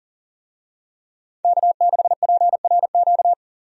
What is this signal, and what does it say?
Morse code, heard as a keyed tone, about 40 wpm, 700 Hz. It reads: K6PRQ
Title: USA Call Signs - 5 Characters - 40wpm